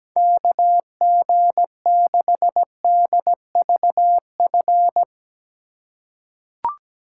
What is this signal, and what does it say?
Morse code, heard as a keyed tone, about 17 wpm, 700 Hz. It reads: KG6DVF
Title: USA Call Signs - 6 Characters - 2x Character Spacing - 17wpm